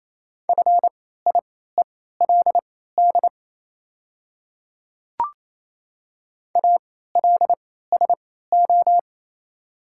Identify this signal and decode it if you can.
Morse code, heard as a keyed tone, about 28 wpm, spaced as for 16 wpm, 700 Hz. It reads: FIELD ALSO
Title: 3x Character Spacing - Top 500 Words - 28wpm